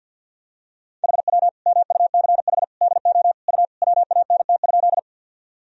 Morse code, heard as a keyed tone, about 51 wpm, 700 Hz. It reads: HW MUCH DO U WANT?
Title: Sentences from Top 200 Words and Abbreviations 50wpm